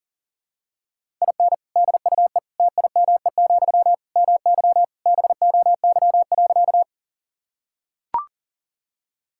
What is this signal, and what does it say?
Morse code, heard as a keyed tone, about 40 wpm, 700 Hz. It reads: IN DUE TIME, MY BOY.
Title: Sentences from Top 1000 Words - Review Part 6 - 40wpm